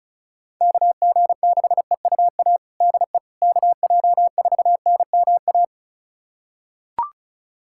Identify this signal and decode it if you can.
Morse code, heard as a keyed tone, about 34 wpm, 700 Hz. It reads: KG6EUA DE KJ4NMA
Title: USA Callsign DE Callsign - 6 Characters - 35wpm